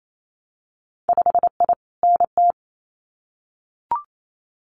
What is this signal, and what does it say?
Morse code, heard as a keyed tone, about 28 wpm, 700 Hz. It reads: 5I NT
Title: Winter Field Day Exchanges Encore 28wpm